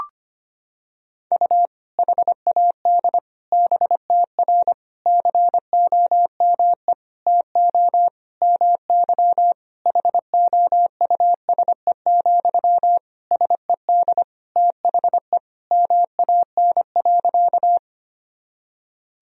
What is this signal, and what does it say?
Morse code, heard as a keyed tone, about 25 wpm, 700 Hz. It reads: U HAD BTR COME TO MY HOUSE, SED THE MAN.
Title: Sentences from Top 400 Words and Abbreviations 25wpm